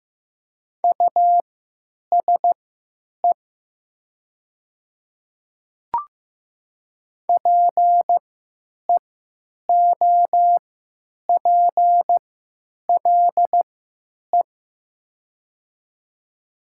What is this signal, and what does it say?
Morse code, heard as a keyed tone, about 15 wpm, spaced as for 9 wpm, 690 Hz. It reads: USE PEOPLE
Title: 3x Character Spacing - Top 100 Words - 15wpm